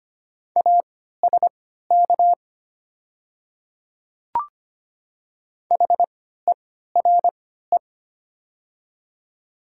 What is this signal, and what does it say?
Morse code, heard as a keyed tone, about 25 wpm, spaced as for 14 wpm, 700 Hz. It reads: ASK HERE
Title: 3x Character Spacing - Top 200 Words - 25wpm